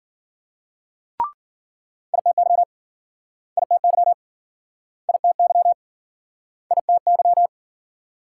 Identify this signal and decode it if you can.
Morse code, heard as a keyed tone, about 53 wpm, 700 Hz. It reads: ITY ITY ITY ITY